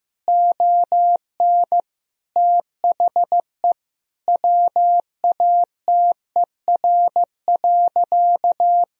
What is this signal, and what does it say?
Morse code, heard as a keyed tone, about 15 wpm, 700 Hz. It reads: ON THE WATER.
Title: Sentences from Top 1000 Words - Review Part 1 - 15wpm